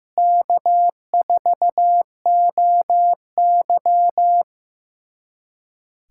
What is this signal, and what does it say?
Morse code, heard as a keyed tone, about 15 wpm, 700 Hz. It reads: K4OY
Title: USA Callsign DE Callsign - 4 Characters - 15wpm